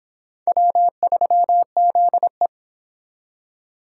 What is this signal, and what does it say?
Morse code, heard as a keyed tone, about 26 wpm, 700 Hz. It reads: W3ZE